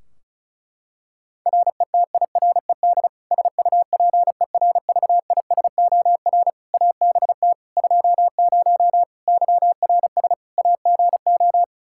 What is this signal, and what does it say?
Morse code, heard as a keyed tone, about 34 wpm, 700 Hz. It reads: RETIRED SUPERVISOR ABT 20 YRS AGO
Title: Practice QSO - Exchange 3 - More Info - 35wpm